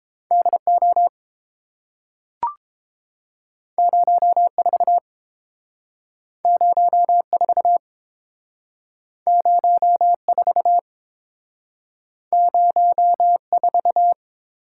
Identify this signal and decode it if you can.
Morse code, heard as a keyed tone, about 30 wpm, 700 Hz. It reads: D O 04 04 04 04